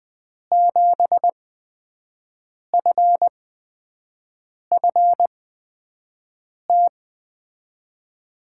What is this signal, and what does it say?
Morse code, heard as a keyed tone, about 20 wpm, spaced as for 5 wpm, 700 Hz. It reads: 7FFT
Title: USA Call Signs - 6 Characters - 8x Character Spacing - 20wpm